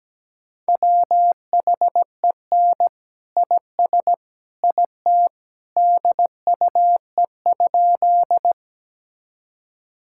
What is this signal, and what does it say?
Morse code, heard as a keyed tone, about 17 wpm, 700 Hz. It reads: WHEN IS IT DUE?
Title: Sentences from Top 1000 Words - Review Part 6 - 17wpm